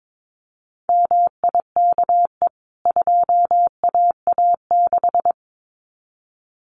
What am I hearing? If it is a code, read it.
Morse code, heard as a keyed tone, about 22 wpm, 700 Hz. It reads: MIKE 2AA6
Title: CWT Exchange 22wpm